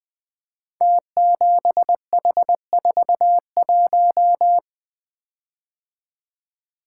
Morse code, heard as a keyed tone, about 20 wpm, 700 Hz. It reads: T7H41
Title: Code Groups - 5 Letter:Numbers - 20wpm